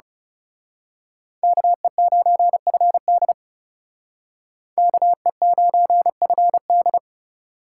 Morse code, heard as a keyed tone, about 34 wpm, 700 Hz. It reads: KE9FD KE9FD
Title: Speed Racing - USA 2x2 Call Signs 30wpm